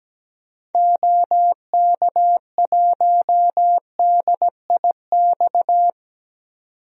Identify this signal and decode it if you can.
Morse code, heard as a keyed tone, about 17 wpm, 700 Hz. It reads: OK1DIX
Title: DXCC-100 Call Signs - 6 Characters - 17wpm